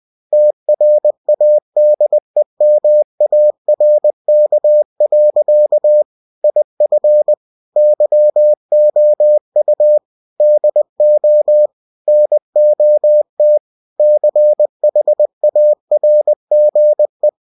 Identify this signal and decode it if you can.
Morse code, heard as a keyed tone, about 20 wpm, 600 Hz. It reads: TRADEMARK. IF YOU DO NOT CHARGE
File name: cw-books / winnie-the-pooh / winnie-20wpm-page-0112